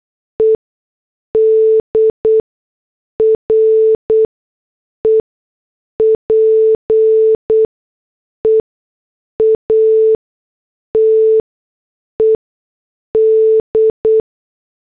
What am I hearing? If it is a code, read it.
Morse code, heard as a keyed tone, about 8 wpm, spaced as for 6 wpm, 440 Hz. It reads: ED REPEATED
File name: morse / headlines